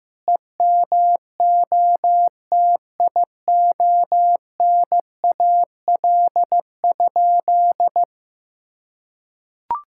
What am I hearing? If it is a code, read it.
Morse code, heard as a keyed tone, about 15 wpm, 700 Hz. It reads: EMOTIONAL?
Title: Sentences from Top 1000 Words - Review Part 5 - 15wpm